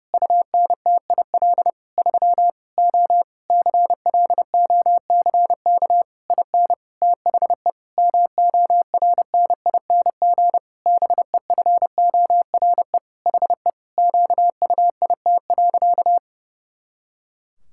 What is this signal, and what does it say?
Morse code, heard as a keyed tone, about 30 wpm, 700 Hz. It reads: UNTIL 3 O CLOCK IN THE MORNING BEFORE HE QUIT.
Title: Pangram Challenge v2 - 30wpm